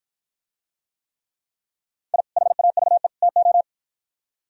Morse code, heard as a keyed tone, about 53 wpm, 700 Hz. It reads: I HAVE TO